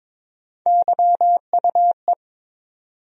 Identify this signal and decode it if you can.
Morse code, heard as a keyed tone, about 22 wpm, 700 Hz. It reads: YUE
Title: USA Call Sign Postfixes - 3 Letters 22wpm